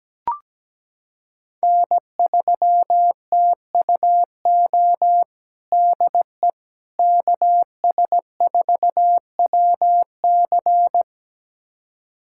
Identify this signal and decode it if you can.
Morse code, heard as a keyed tone, about 17 wpm, 700 Hz. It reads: N3TUO DE KS4WC
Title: USA Callsign DE Callsign - 5 Characters - 17wpm